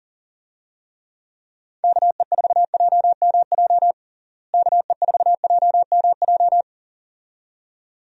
Morse code, heard as a keyed tone, about 40 wpm, 700 Hz. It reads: KE4JMJ KE4JMJ
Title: USA Call Signs - 6 Characters - Repeated 2x - 40wpm